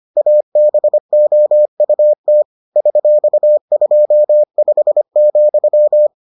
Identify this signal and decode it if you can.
Morse code, heard as a keyed tone, about 25 wpm, 600 Hz. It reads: ABOUT $25,